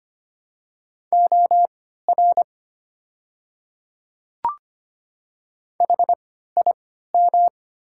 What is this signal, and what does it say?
Morse code, heard as a keyed tone, about 25 wpm, spaced as for 14 wpm, 690 Hz. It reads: OR HIM
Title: 3x Character Spacing - Top 200 Words - 25wpm